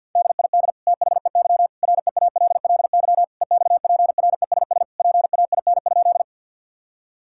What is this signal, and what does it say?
Morse code, heard as a keyed tone, about 50 wpm, 700 Hz. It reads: DID THEY REALLY EXPRESS PAIN?
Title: Sentences from Top 1600 Words 50wpm